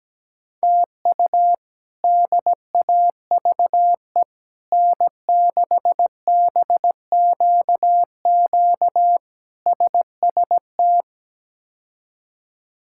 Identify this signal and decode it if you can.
Morse code, heard as a keyed tone, about 17 wpm, 700 Hz. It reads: TU DAVE N6BQQ SST
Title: K1USN SST QSO Practice - 17wpm